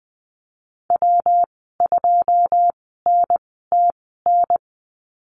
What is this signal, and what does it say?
Morse code, heard as a keyed tone, about 20 wpm, spaced as for 15 wpm, 700 Hz. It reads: W2NTN
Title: 179 Lesson 29g - Slash - Call Signs 20wpm